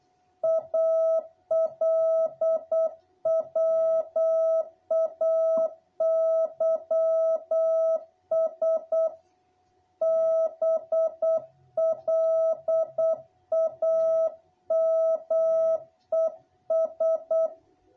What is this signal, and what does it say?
Morse code, heard as a keyed tone, about 8 wpm, 660 Hz. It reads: ALWAYS BLAMES
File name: jh8cbh / 2021cw